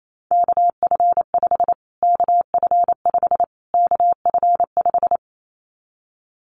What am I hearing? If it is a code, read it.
Morse code, heard as a keyed tone, about 28 wpm, 700 Hz. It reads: KF5 KF5 KF5
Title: Callsign Prefixes for Top 25 DXCC - Repeated 3x - 28wpm